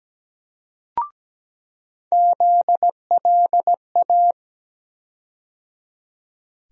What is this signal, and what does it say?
Morse code, heard as a keyed tone, about 17 wpm, 700 Hz. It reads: ZLA
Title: USA Call Sign Postfixes - 3 Letters 17wpm